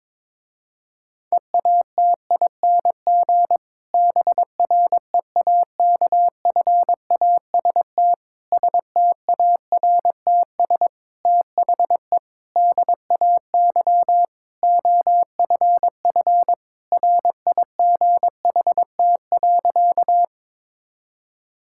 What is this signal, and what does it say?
Morse code, heard as a keyed tone, about 22 wpm, 700 Hz. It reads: EATING BREAKFAST STARTS THE DAY OFF RIGHT.
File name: Difficult Sentences with Hints - 22wpm